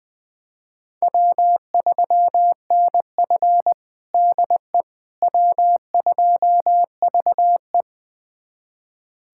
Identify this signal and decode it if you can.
Morse code, heard as a keyed tone, about 20 wpm, 700 Hz. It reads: W3NF DE W2VE
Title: USA Callsign DE Callsign - 4 Characters - 20wpm